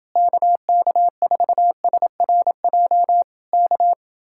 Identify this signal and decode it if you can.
Morse code, heard as a keyed tone, about 28 wpm, 700 Hz. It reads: KK4SRJ K